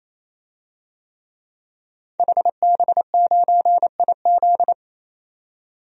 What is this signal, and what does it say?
Morse code, heard as a keyed tone, about 27 wpm, 700 Hz. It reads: HB9IZ
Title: DXCC-100 Call Signs - 5 Characters - 28wpm